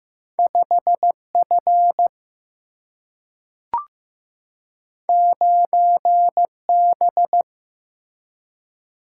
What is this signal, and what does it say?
Morse code, heard as a keyed tone, about 15 wpm, 700 Hz. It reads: 5F 9B
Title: Single-Letter Single-Number 15wpm